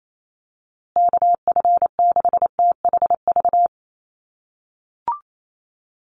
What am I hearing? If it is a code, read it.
Morse code, heard as a keyed tone, about 28 wpm, 700 Hz. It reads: KF6THV
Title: USA Call Signs Repeated 2x 28wpm